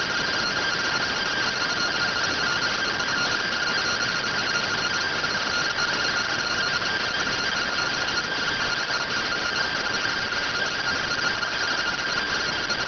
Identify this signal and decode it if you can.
Two-tone teleprinter signal, about 170 Hz shift.